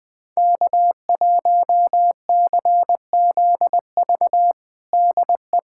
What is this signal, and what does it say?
Morse code, heard as a keyed tone, about 21 wpm, 700 Hz. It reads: K1CZV DE
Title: USA Callsign DE Callsign - 5 Characters - 20wpm